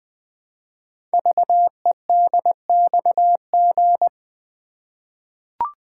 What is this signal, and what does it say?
Morse code, heard as a keyed tone, about 20 wpm, 700 Hz. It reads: VEDXG
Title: Code Groups - 5 Letters - 20wpm